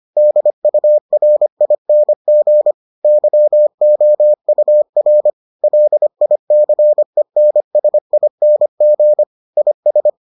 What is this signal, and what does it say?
Morse code, heard as a keyed tone, about 25 wpm, 600 Hz. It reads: DURING YOUR LICENSING IS